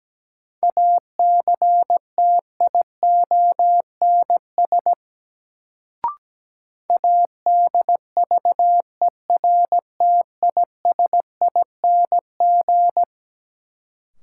Morse code, heard as a keyed tone, about 17 wpm, 700 Hz. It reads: ACTIONS ADVERTISING
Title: Sentences from Top 1000 Words - Review Part 6 - New Word Drill - 17wpm